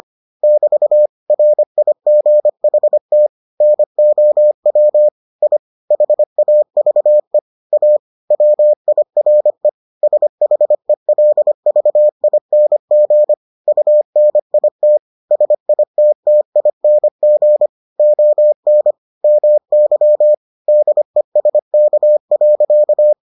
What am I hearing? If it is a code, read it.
Morse code, heard as a keyed tone, about 25 wpm, 600 Hz. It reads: <BT> RIGHT NOW I HAVE A WIRE SHELVING UNIT SITTING ON MY DESK.